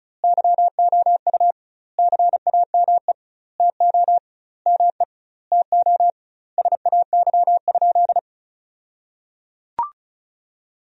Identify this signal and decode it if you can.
Morse code, heard as a keyed tone, about 36 wpm, 700 Hz. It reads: YOU CAME TO ME TO SAY?
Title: 2x Word-Spacing - Sentences from Top 200 Words - 35wpm